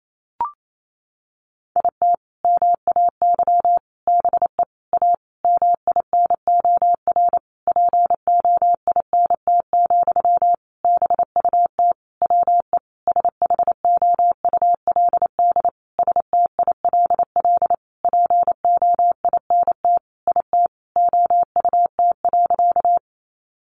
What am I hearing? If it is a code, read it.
Morse code, heard as a keyed tone, about 28 wpm, 700 Hz. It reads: IT MAY BE A MINOR POINT, BUT WE SHOULD STILL POINT IT OUT.
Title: Sentences from Top 1500 Words 28wpm